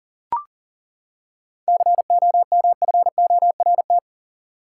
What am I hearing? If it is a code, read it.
Morse code, heard as a keyed tone, about 40 wpm, 700 Hz. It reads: COMFORT
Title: Seven Letter Words - 40wpm